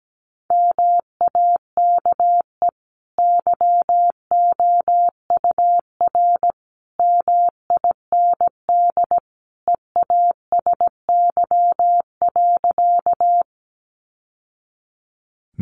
Morse code, heard as a keyed tone, about 17 wpm, 700 Hz. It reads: MAKE YOUR MIND EASY.
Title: Sentences from Top 700 Words 17wpm